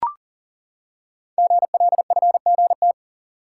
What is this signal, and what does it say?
Morse code, heard as a keyed tone, about 40 wpm, 700 Hz. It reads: GLFGT
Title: Code Groups - 5 Letters - 40wpm